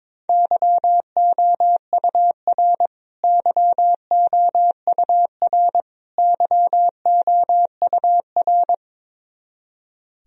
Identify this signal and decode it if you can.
Morse code, heard as a keyed tone, about 22 wpm, 700 Hz. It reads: YOUR YOUR YOUR
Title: Top 100 Words Repeated 22wpm